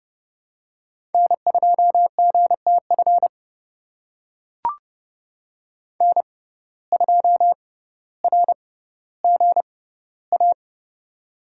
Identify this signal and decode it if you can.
Morse code, heard as a keyed tone, about 31 wpm, spaced as for 15 wpm, 700 Hz. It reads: N2GTF N2RGA
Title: USA Call Signs - 5 Characters - 6x Character Spacing - 30wpm